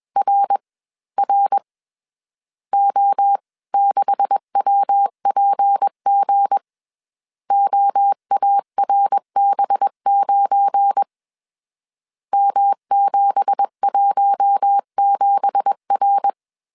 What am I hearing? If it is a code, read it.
Morse code, heard as a keyed tone, about 21 wpm, spaced as for 18 wpm, 800 Hz. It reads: RR O6WPG OARB9 M717R